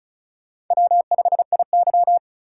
Morse code, heard as a keyed tone, about 36 wpm, 700 Hz. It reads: W5IY
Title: USA Call Signs - 4 Characters - 4x Character Spacing - 35wpm